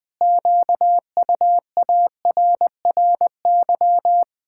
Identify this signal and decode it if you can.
Morse code, heard as a keyed tone, about 20 wpm, 700 Hz. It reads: QUARRY
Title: Pangram Challenge v2 - 20wpm